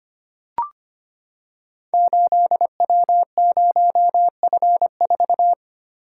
Morse code, heard as a keyed tone, about 25 wpm, 700 Hz. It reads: 8W0F4
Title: Code Groups - 5 Letter:Numbers - 25wpm